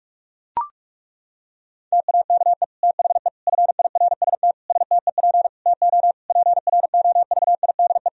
Morse code, heard as a keyed tone, about 41 wpm, 700 Hz. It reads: TAKE THE FIRST STEP TO PROVIDE
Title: Sentences from Top 1000 Words - Review Part 4 - 45wpm